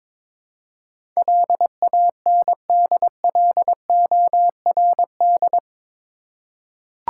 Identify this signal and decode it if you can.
Morse code, heard as a keyed tone, about 21 wpm, 700 Hz. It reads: LANDLORD
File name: practice / 22wpm / Eight Letter Words - 22wpm